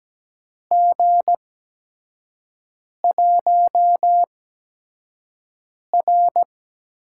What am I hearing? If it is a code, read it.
Morse code, heard as a keyed tone, about 17 wpm, spaced as for 5 wpm, 700 Hz. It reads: G1R
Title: USA Call Signs - 8x Character Spacing - 17wpm